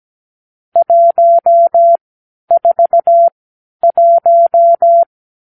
Morse code, heard as a keyed tone, about 17 wpm, spaced as for 10 wpm, 680 Hz. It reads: 141
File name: k7qo / Lesson 46 (1-4)